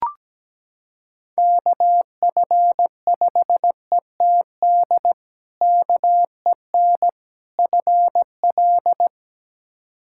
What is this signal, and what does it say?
Morse code, heard as a keyed tone, about 17 wpm, 700 Hz. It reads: KF5ETD KEN FL
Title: K1USN SST QSO Practice - 17wpm